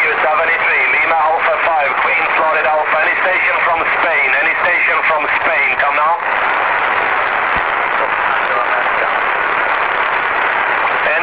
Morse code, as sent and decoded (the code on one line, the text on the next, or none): none